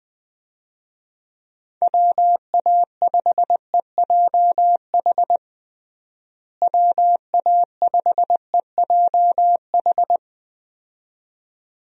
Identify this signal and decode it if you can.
Morse code, heard as a keyed tone, about 21 wpm, 700 Hz. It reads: WA5EJH WA5EJH
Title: USA Call Signs - 6 Characters - Repeated 2x - 20wpm